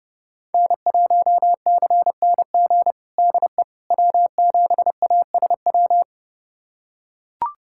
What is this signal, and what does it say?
Morse code, heard as a keyed tone, about 32 wpm, 700 Hz. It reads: N1CNG DE W7ASW
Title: USA Callsign DE Callsign - 5 Characters - 30wpm